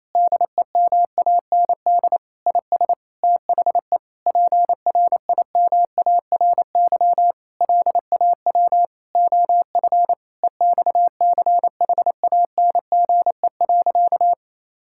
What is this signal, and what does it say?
Morse code, heard as a keyed tone, about 27 wpm, 700 Hz. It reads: DEMAND IS THE PRIMARY LAW OF EXCHANGE.
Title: Sentences from Top 1000 Words - Review Part 6 - 28wpm